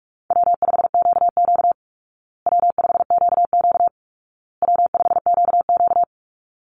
Morse code, heard as a keyed tone, about 44 wpm, 700 Hz. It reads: W5QQ W5QQ W5QQ
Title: Easy Call Signs Repeated 3x - 45wpm